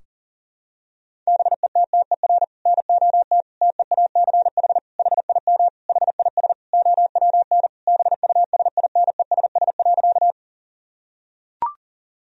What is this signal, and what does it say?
Morse code, heard as a keyed tone, about 37 wpm, 700 Hz. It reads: BETTER NOT TEACH HIM HIS OWN BUSINESS.
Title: Sentences from Top 1000 Words - Review Part 2 - 40wpm